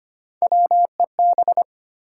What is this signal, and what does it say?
Morse code, heard as a keyed tone, about 25 wpm, 700 Hz. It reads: WEB
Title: Sentences from Top 1000 Words - Review Part 4 - New Word Drill - 25wpm